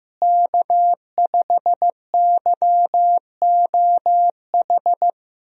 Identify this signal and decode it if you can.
Morse code, heard as a keyed tone, about 15 wpm, 700 Hz. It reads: K5YOH
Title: USA Call Signs - 5 Characters - Repeated 2x - 15wpm